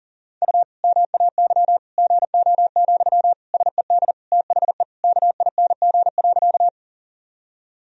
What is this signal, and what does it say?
Morse code, heard as a keyed tone, about 40 wpm, 700 Hz. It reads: U MAY GO, SED THE KING.
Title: Sentences from Top 400 Words and Abbreviations 40wpm